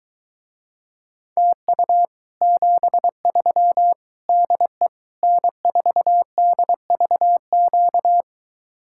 Morse code, heard as a keyed tone, about 23 wpm, 700 Hz. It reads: TU 73 DE N4DVQ